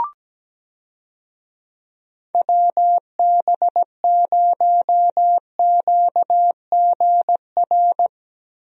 Morse code, heard as a keyed tone, about 17 wpm, 700 Hz. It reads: WB0QGR
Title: USA Call Signs - 6 Characters - 17wpm